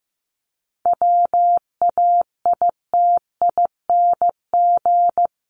Morse code, heard as a keyed tone, about 15 wpm, 700 Hz. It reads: WAITING
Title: ING Suffix - Top 100 Words with Base Words 15wpm